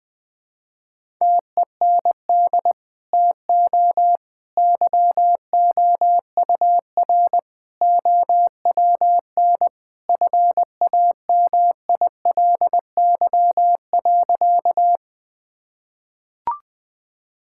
Morse code, heard as a keyed tone, about 20 wpm, 700 Hz. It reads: TEND TO YOUR OWN FAMILY.
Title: Sentences from Top 1000 Words - Review Part 5 - 20wpm